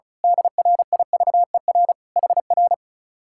Morse code, heard as a keyed tone, about 33 wpm, 700 Hz. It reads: DRIVER HR
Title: Speed-Racing Common QSO Phrases - 35wpm